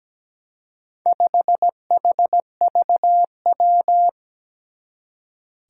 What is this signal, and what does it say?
Morse code, heard as a keyed tone, about 17 wpm, 700 Hz. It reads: 5HVW
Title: Four Challenging Letter-Number Combinations 17wpm